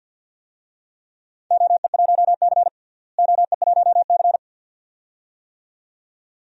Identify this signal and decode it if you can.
Morse code, heard as a keyed tone, about 51 wpm, 700 Hz. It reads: OE1C OE1C
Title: DXCC-100 Call Signs - 4 Characters - Repeated 2x - 50wpm